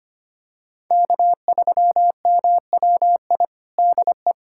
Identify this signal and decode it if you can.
Morse code, heard as a keyed tone, about 25 wpm, 700 Hz. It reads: K3MWI DE
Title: USA Callsign DE Callsign 25wpm